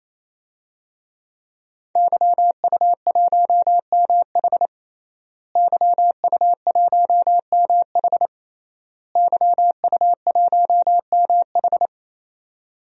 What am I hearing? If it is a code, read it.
Morse code, heard as a keyed tone, about 27 wpm, 700 Hz. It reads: YU1MH YU1MH YU1MH
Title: DXCC-100 Call Signs - 5 Characters - Repeated 3x - 28wpm